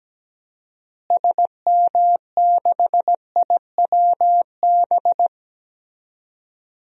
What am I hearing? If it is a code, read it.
Morse code, heard as a keyed tone, about 17 wpm, 700 Hz. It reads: SM6IWB
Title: DXCC-100 Call Signs - 6 Characters - Repeated 2x - 17wpm